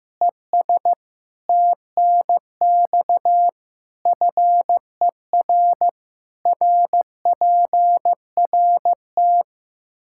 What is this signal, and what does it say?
Morse code, heard as a keyed tone, about 15 wpm, 700 Hz. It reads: ES TNX FER RPRT
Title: K1USN SST QSO Practice 15wpm